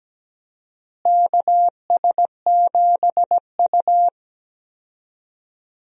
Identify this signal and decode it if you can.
Morse code, heard as a keyed tone, about 17 wpm, 700 Hz. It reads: KS7U
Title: USA Call Signs - 4 Characters - Repeated 3x - 17wpm